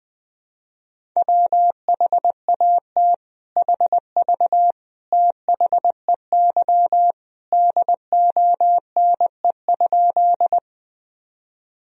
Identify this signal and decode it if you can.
Morse code, heard as a keyed tone, about 21 wpm, 700 Hz. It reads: WHAT HV THEY DONE?
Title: Sentences from Top 500 Words and Abbreviations 20wpm